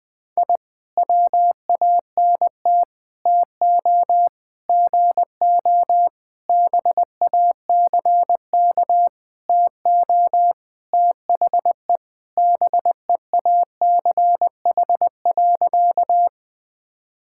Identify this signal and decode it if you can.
Morse code, heard as a keyed tone, about 21 wpm, 700 Hz. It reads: I WANT TO GO BACK TO THE BEACH.
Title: Sentences with 5-Word Phrases - Spoken Beginning - 20wpm